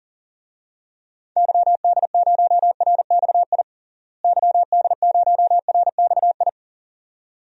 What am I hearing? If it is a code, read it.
Morse code, heard as a keyed tone, about 40 wpm, 700 Hz. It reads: YD0RXI YD0RXI
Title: DXCC-100 Call Signs - 6 Characters - Repeated 2x - 40wpm